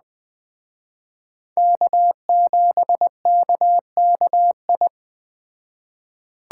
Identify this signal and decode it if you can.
Morse code, heard as a keyed tone, about 20 wpm, 700 Hz. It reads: K7KKI